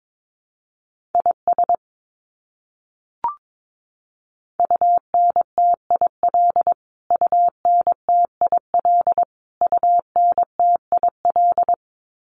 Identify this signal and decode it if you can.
Morse code, heard as a keyed tone, about 22 wpm, 700 Hz. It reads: IS UNTIL UNTIL UNTIL
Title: Top 500 Words Repeated 22wpm